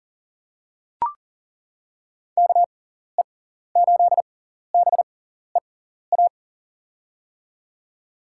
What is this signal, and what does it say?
Morse code, heard as a keyed tone, about 39 wpm, spaced as for 14 wpm, 700 Hz. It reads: KE8BEA
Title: USA Call Signs - 6 Characters - 6x Character Spacing - 40wpm